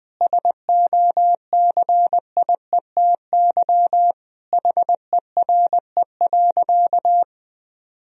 Morse code, heard as a keyed tone, about 21 wpm, 700 Hz. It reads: SOCIETY HERE.
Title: Sentences from Top 1000 Words - Review Part 4 - 20wpm